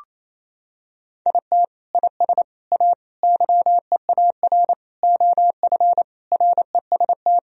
Morse code, heard as a keyed tone, about 28 wpm, 700 Hz. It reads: IT IS A YEAR OF REST
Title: Sentences from Top 1000 Words - Review Part 1 - 28wpm